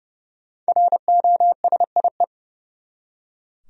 Morse code, heard as a keyed tone, about 30 wpm, 700 Hz. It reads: ROSIE
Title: USA Common Names - 30wpm